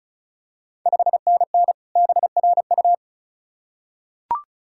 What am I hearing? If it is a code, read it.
Morse code, heard as a keyed tone, about 36 wpm, 690 Hz. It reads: 5NN BRU
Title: Ontario QSO Party Exchange 35wpm